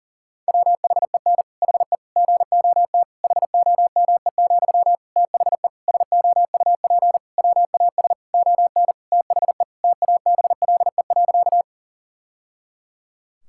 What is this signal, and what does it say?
Morse code, heard as a keyed tone, about 40 wpm, 700 Hz. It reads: WHEN HE GOT HOME, THE SOUP WAS ON THE TABLE.
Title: Sentences from Top 1000 Words - Review Part 6 - 40wpm